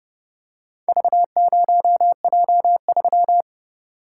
Morse code, heard as a keyed tone, about 30 wpm, 700 Hz. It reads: V0J3